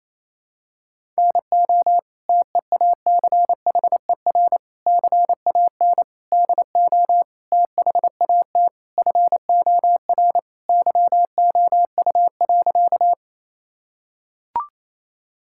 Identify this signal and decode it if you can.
Morse code, heard as a keyed tone, about 28 wpm, 700 Hz. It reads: NO TEACHER CAN DO THAT FOR YOU.
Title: Sentences from Top 1000 Words - Review Part 6 - 28wpm